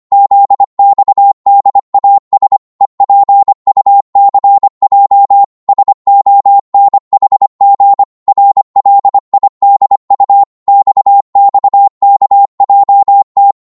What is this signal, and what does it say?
Morse code, heard as a keyed tone, about 25 wpm, 800 Hz. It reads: ZXDAS EPUCJ SONHG RLIDU XXKJT
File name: Morse / ltr-25wpm